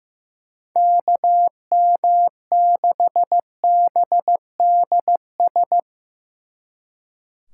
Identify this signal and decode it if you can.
Morse code, heard as a keyed tone, about 15 wpm, 700 Hz. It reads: KM6BDS
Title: Speed Racing - USA 2x3 Call Signs 15wpm